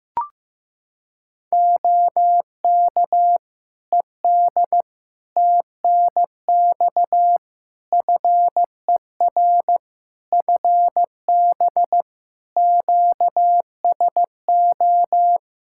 Morse code, heard as a keyed tone, about 16 wpm, 700 Hz. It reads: OK ED TNX FER FB QSO
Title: Practice QSO - Exchange 4 - Ending - 15wpm